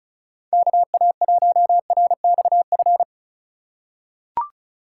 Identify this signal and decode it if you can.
Morse code, heard as a keyed tone, about 36 wpm, 700 Hz. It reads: KA1RXF
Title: USA Call Signs - 6 Characters - 4x Character Spacing - 35wpm